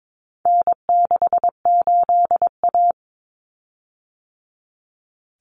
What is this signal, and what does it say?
Morse code, heard as a keyed tone, about 22 wpm, 700 Hz. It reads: N68A
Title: Common Cut Numbers - 4-Digit 22wpm